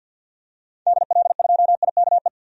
Morse code, heard as a keyed tone, about 49 wpm, 700 Hz. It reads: DL2IKE
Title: DXCC-100 Call Signs - 6 Characters - 50wpm